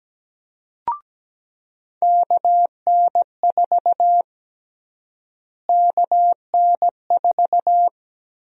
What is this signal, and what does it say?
Morse code, heard as a keyed tone, about 17 wpm, 700 Hz. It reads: KN4 KN4
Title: USA Call Sign Prefixes - Repeated 2x - 17wpm